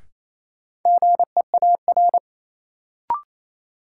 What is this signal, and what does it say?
Morse code, heard as a keyed tone, about 27 wpm, 700 Hz. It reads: GEAR
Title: Sentences from Top 1400 Words - New Word Drill - 28wpm